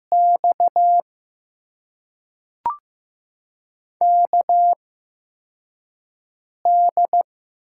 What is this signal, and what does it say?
Morse code, heard as a keyed tone, about 15 wpm, spaced as for 4 wpm, 700 Hz. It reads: X KD
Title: USA Call Signs - 6 Characters - 8x Character Spacing - 15wpm